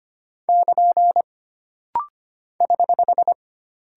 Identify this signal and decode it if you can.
Morse code, heard as a keyed tone, about 25 wpm, 700 Hz. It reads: <KN> <HH>
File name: Single Punctuation-Prosign - Rapid-Fire - 25wpm